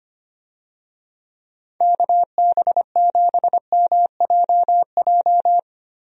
Morse code, heard as a keyed tone, about 25 wpm, 700 Hz. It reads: KB7MJJ
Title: USA Call Signs - 6 Characters - Repeated 3x - 25wpm